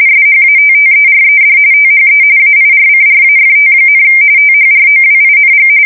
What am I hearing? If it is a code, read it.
RTTY, about 45.45 baud. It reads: SHATTERED THE AIRCRAFT, THROWING
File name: fsk170 / USS-Missouri